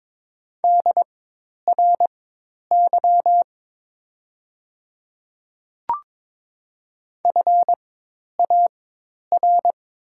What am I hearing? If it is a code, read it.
Morse code, heard as a keyed tone, about 22 wpm, spaced as for 10 wpm, 700 Hz. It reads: DRY FAR
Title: 4x Character Spacing - Top 500 Words - 22wpm